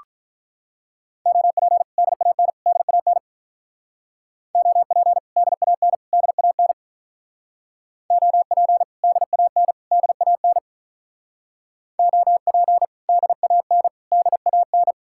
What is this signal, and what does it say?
Morse code, heard as a keyed tone, about 45 wpm, 700 Hz. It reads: OP DAN DAN OP DAN DAN OP DAN DAN OP DAN DAN